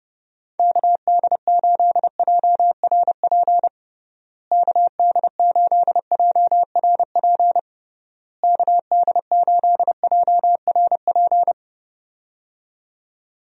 Morse code, heard as a keyed tone, about 30 wpm, 700 Hz. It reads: KD8JRP KD8JRP KD8JRP
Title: USA Call Signs - 6 Characters - Repeated 3x - 30wpm